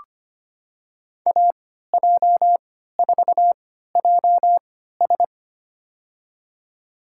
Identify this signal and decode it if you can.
Morse code, heard as a keyed tone, about 25 wpm, spaced as for 14 wpm, 700 Hz. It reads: AJ4JS
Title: USA Call Signs - 5 Characters - 3x Character Spacing - 25wpm